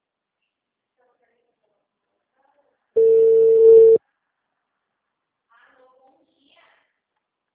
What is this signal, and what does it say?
Ringback tone (China, 450 Hz), with speech.